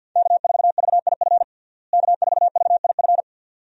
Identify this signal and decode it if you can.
Morse code, heard as a keyed tone, about 52 wpm, 700 Hz. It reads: K4VIF K4VIF
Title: USA Call Signs - 5 Characters - Repeated 2x - 50wpm